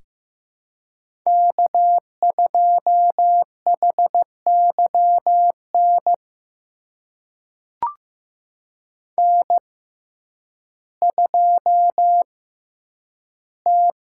Morse code, heard as a keyed tone, about 15 wpm, spaced as for 9 wpm, 700 Hz. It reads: K2HYN N2T
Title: USA Call Signs - 5 Characters - 6x Character Spacing - 15wpm